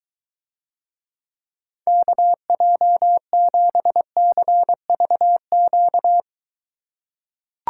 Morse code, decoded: KJ7CVQ

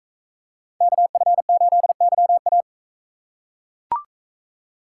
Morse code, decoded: KF8YA

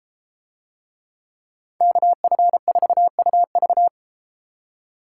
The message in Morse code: KF4UV

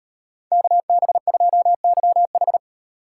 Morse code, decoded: KB2YH

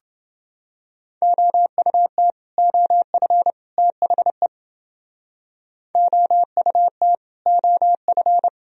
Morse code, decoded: OUT OF THE OUT OF

